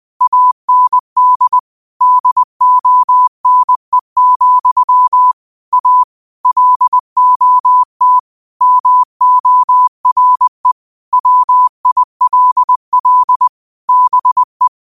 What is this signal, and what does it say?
Morse code, heard as a keyed tone, about 20 wpm, 1000 Hz. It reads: AND DONE, A LOT MORE WILL BE
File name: qotdfiles / 20wpm